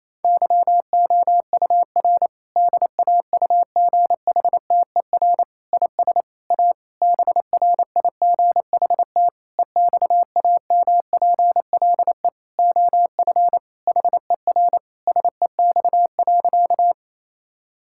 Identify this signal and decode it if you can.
Morse code, heard as a keyed tone, about 28 wpm, 700 Hz. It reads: YOUR DAUGHTER IS A BRIGHT EXAMPLE OF HER SEX.
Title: Sentences from Top 1600 Words 28wpm